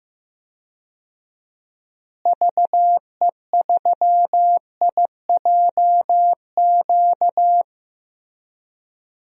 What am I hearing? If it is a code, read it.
Morse code, heard as a keyed tone, about 15 wpm, 700 Hz. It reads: VE3IJQ
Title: DXCC-100 Call Signs - 6 Characters - Repeated 2x - 15wpm